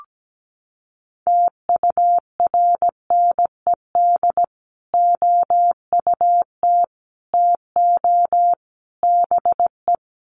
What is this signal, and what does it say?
Morse code, heard as a keyed tone, about 17 wpm, 700 Hz. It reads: TURNED OUT TO BE
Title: Sets of 4 Words 17wpm